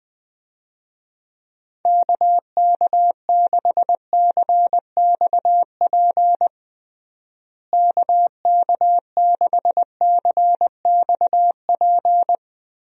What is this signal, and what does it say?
Morse code, heard as a keyed tone, about 21 wpm, 700 Hz. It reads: KK6CXP KK6CXP
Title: USA Call Signs - 6 Characters - Repeated 2x - 20wpm